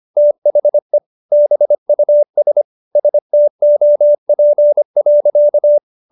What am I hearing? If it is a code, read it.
Morse code, heard as a keyed tone, about 25 wpm, 590 Hz. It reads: THE BUS STOP.